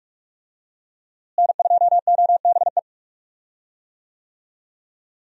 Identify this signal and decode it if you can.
Morse code, heard as a keyed tone, about 46 wpm, 700 Hz. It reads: N2OBE
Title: USA Call Signs - 5 Characters - Repeated 3x - 45wpm